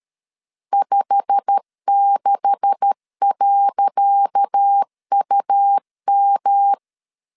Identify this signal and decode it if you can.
Morse code, heard as a keyed tone, about 13 wpm, 800 Hz. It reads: 56.UM
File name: LEZIONI / 10 SIGNS / 10C DOT 12wpm 2451